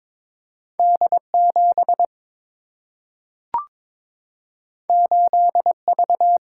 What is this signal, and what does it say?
Morse code, heard as a keyed tone, about 22 wpm, 700 Hz. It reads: D 7 8 V
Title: Single-Letter Single-Number 22wpm